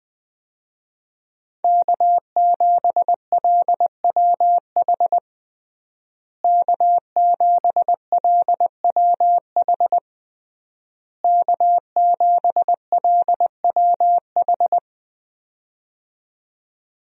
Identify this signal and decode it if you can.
Morse code, heard as a keyed tone, about 21 wpm, 700 Hz. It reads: K7LWH K7LWH K7LWH
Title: USA Call Signs - 5 Characters - Repeated 3x - 20wpm